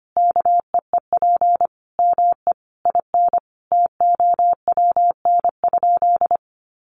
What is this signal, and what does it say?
Morse code, heard as a keyed tone, about 25 wpm, 700 Hz. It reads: KEEP ME IN TOWN?
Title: Sentences from Top 500 Words 25wpm